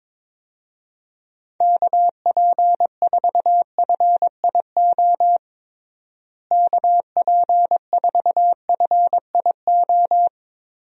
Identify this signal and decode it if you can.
Morse code, heard as a keyed tone, about 22 wpm, 700 Hz. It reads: KP4FIO KP4FIO
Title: USA Call Signs - 6 Characters - Repeated 3x - 22wpm